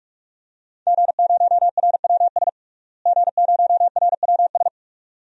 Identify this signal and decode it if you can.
Morse code, heard as a keyed tone, about 46 wpm, 700 Hz. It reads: G0RWS G0RWS
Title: DXCC-100 Call Signs - 5 Characters - Repeated 2x - 45wpm